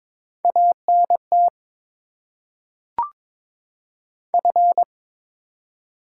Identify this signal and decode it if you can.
Morse code, heard as a keyed tone, about 22 wpm, 700 Hz. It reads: ANT F
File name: Top 20 QSO Elements - 8x Character Spacing - 22wpm